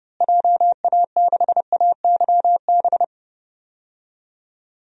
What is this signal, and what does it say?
Morse code, heard as a keyed tone, about 30 wpm, 700 Hz. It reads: JA6AYB
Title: DXCC-100 Call Signs - 6 Characters - Repeated 2x - 30wpm